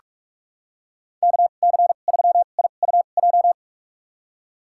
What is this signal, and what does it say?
Morse code, heard as a keyed tone, about 45 wpm, spaced as for 32 wpm, 700 Hz. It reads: KC3IUJ
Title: USA Call Signs - 6 Characters - 2x Character Spacing - 45wpm